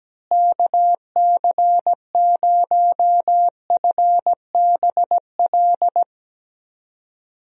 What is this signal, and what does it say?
Morse code, heard as a keyed tone, about 17 wpm, 700 Hz. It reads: KC0FBL
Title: USA Call Signs - 6 Characters - Repeated 2x - 17wpm